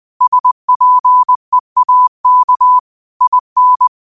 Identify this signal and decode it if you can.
Morse code, heard as a keyed tone, about 20 wpm, 1000 Hz. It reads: SPEAK IN
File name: 20wpm